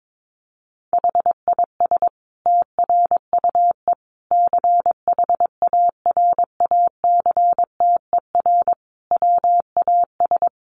Morse code, heard as a keyed tone, about 22 wpm, 700 Hz. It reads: HIS TRUE CHARACTER WAS
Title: Sentences from Top 1200 Words 22wpm